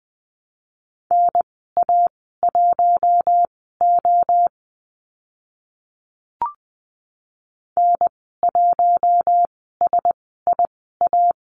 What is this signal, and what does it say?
Morse code, heard as a keyed tone, about 20 wpm, spaced as for 14 wpm, 700 Hz. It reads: NA1O N1SIA
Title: 042 Lesson 8g - Number 1 - Call Signs 20wpm